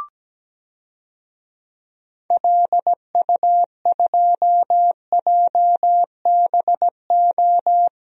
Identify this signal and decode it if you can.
Morse code, heard as a keyed tone, about 17 wpm, 700 Hz. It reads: LU2JBO
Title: DXCC-100 Call Signs - 6 Characters - 17wpm